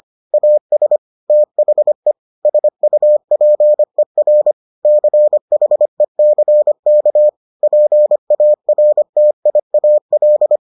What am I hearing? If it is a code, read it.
Morse code, heard as a keyed tone, about 25 wpm, 600 Hz. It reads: AS THE SUPER CHECK PARTIAL